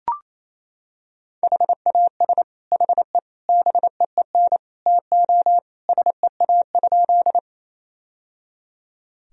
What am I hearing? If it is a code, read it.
Morse code, heard as a keyed tone, about 28 wpm, 700 Hz. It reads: HAS HE BEEN TO SEA?
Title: Sentences from Top 1000 Words - Review Part 1 - 28wpm